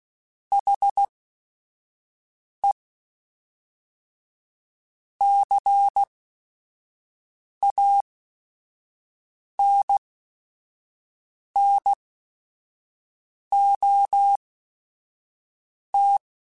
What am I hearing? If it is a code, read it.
Morse code, heard as a keyed tone, about 16 wpm, spaced as for 5 wpm, 800 Hz. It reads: HE CANNOT